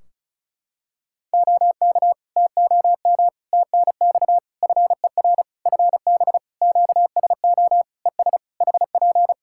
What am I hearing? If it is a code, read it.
Morse code, heard as a keyed tone, about 34 wpm, 700 Hz. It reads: OK TOM TNX FER FB QSO ES HP
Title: Practice QSO - Exchange 4 - Ending - 35wpm